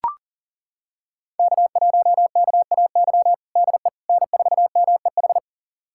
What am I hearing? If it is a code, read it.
Morse code, heard as a keyed tone, about 37 wpm, 700 Hz. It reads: K1KAY DE N4MEH
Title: USA Callsign DE Callsign - 5 Characters - 40wpm